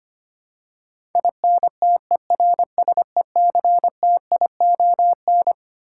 Morse code, heard as a keyed tone, about 25 wpm, 700 Hz. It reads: INTERSECTION
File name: Twelve Letter Words - 25wpm